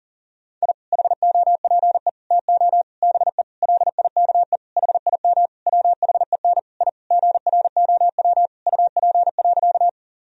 Morse code, heard as a keyed tone, about 40 wpm, 700 Hz. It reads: I HOPE TO BE LIKE HIM WHEN I GROW UP.
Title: Sentences from Top 1000 Words - Review Part 1 - 40wpm